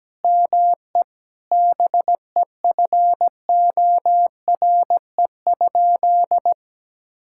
Morse code, heard as a keyed tone, about 17 wpm, 700 Hz. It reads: ME BEFORE?
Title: Sentences from Top 1000 Words - Review Part 6 - 17wpm